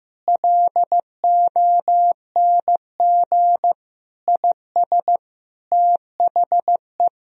Morse code, heard as a keyed tone, about 15 wpm, 700 Hz. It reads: LONG IS THE
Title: Sentences from Top 1000 Words - Review Part 4 - 15wpm